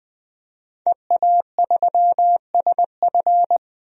Morse code, heard as a keyed tone, about 20 wpm, 700 Hz. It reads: EA3SF
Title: DXCC-100 Call Signs - 5 Characters - 20wpm